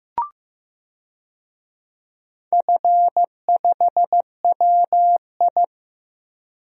Morse code, heard as a keyed tone, about 15 wpm, 700 Hz. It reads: F5WI
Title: DXCC-100 Call Signs - 4 Characters - Repeated 3x - 15wpm